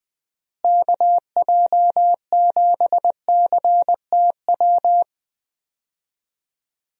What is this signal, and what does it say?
Morse code, heard as a keyed tone, about 20 wpm, 700 Hz. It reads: KJ7CTW
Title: USA Call Signs - 6 Characters - Repeated 3x - 20wpm